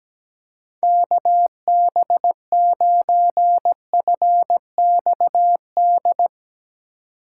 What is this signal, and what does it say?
Morse code, heard as a keyed tone, about 17 wpm, 700 Hz. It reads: KB9FXD
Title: USA Call Signs - 6 Characters - 4x Character Spacing - 17wpm